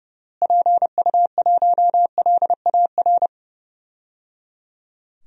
Morse code, heard as a keyed tone, about 30 wpm, 700 Hz. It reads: PU1LAR
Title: DXCC-100 Call Signs - 6 Characters - Repeated 3x - 30wpm